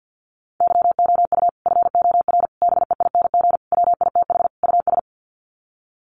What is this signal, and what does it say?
Morse code, heard as a keyed tone, about 50 wpm, 700 Hz. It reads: YOU FOR BEING WITH US
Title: Sets of 5 Words 50wpm